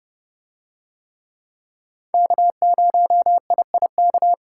Morse code, heard as a keyed tone, about 29 wpm, 700 Hz. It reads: K0IIK